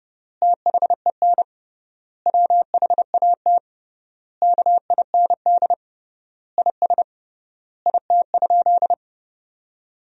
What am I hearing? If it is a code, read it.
Morse code, heard as a keyed tone, about 30 wpm, 700 Hz. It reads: THEN WHAT KIND IS IT?
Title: 3x Word-Spacing - Sentences from Top 200 Words - 30wpm